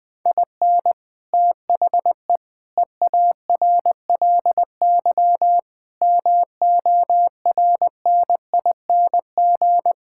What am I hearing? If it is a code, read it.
Morse code, heard as a keyed tone, about 20 wpm, 700 Hz. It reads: IN THE EARLY MORNING
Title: Sentences with 6-Word Phrases - 20wpm